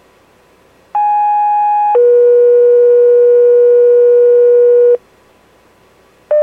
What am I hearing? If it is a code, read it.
Two-tone page, heard as a one-second tone, then a three-second tone, about 830 then 480 Hz. It is heard from 0.9 to 5.0 s.